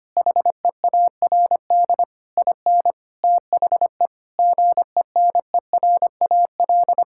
Morse code, heard as a keyed tone, about 25 wpm, 700 Hz. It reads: HEARD IN THE GENERAL
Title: Sentences from Top 1000 Words - Review Part 5 - 25wpm